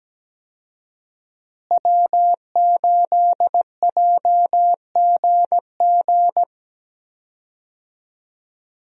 Morse code, heard as a keyed tone, about 17 wpm, 700 Hz. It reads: W8JGG
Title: USA Call Signs - 5 Characters - Repeated 3x - 17wpm